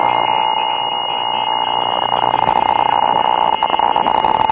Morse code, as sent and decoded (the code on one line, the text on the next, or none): none